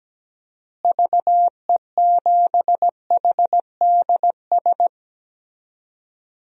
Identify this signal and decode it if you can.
Morse code, heard as a keyed tone, about 17 wpm, 700 Hz. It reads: VE7HDS
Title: DXCC-100 Call Signs - 6 Characters - Repeated 2x - 17wpm